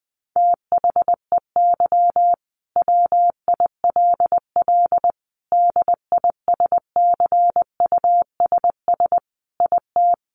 Morse code, heard as a keyed tone, about 20 wpm, 700 Hz. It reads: THEY WILL DISCUSS IT
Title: Sentences from Top 1400 Words 20wpm